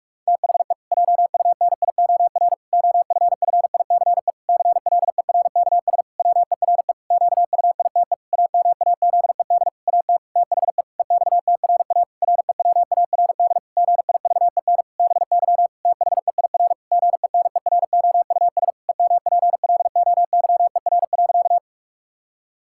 Morse code, heard as a keyed tone, about 41 wpm, 700 Hz. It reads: THE JUNIOR OFFICE CLERKS WERE QUITE AMAZED AT THE EXTRA REWARD GIVEN BY THEIR GENEROUS EMPLOYER.
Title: Pangram Challenge v2 - 45wpm